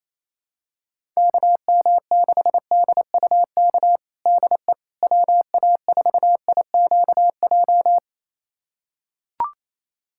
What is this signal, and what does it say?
Morse code, heard as a keyed tone, about 28 wpm, 700 Hz. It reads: KM6DUK DE WA4IQJ
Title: USA Callsign DE Callsign - 6 Characters - 28wpm